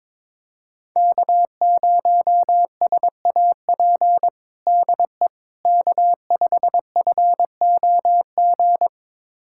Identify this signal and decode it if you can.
Morse code, heard as a keyed tone, about 22 wpm, 700 Hz. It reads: K0SAP DE K5FOG
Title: USA Callsign DE Callsign - 5 Characters - 22wpm